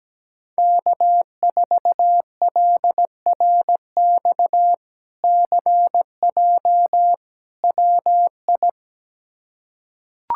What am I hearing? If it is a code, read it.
Morse code, heard as a keyed tone, about 17 wpm, 700 Hz. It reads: K4LRX CJ WI
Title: K1USN SST QSO Practice - 17wpm